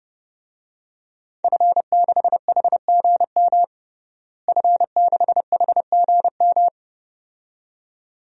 Morse code, heard as a keyed tone, about 30 wpm, 700 Hz. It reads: F6HGM F6HGM
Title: DXCC-100 Call Signs - 5 Characters - Repeated 2x - 30wpm